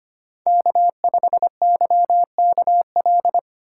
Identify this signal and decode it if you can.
Morse code, heard as a keyed tone, about 25 wpm, 700 Hz. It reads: K5YKL
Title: USA Call Signs - 5 Characters - 2x Character Spacing - 25wpm